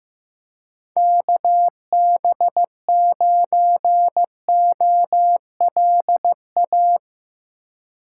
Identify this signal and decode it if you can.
Morse code, heard as a keyed tone, about 15 wpm, 700 Hz. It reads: KB9OLA
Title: USA Call Signs - 4x Character Spacing - 15wpm